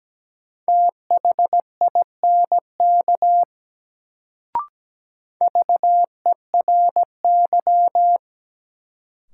Morse code, heard as a keyed tone, about 17 wpm, 700 Hz. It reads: THINK VERY
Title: Sentences from Top 200 Words - New Word Drill - 17wpm